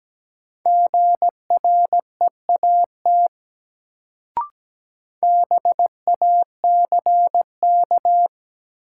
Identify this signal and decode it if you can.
Morse code, heard as a keyed tone, about 17 wpm, 700 Hz. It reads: GREAT BACK
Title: Sentences from Top 200 Words - New Word Drill - 17wpm